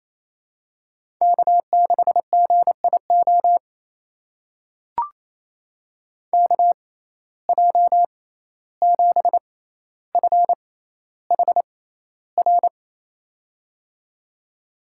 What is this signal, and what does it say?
Morse code, heard as a keyed tone, about 28 wpm, spaced as for 14 wpm, 700 Hz. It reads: K6GIO KJ7FHR
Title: USA Call Signs - 6x Character Spacing - 28wpm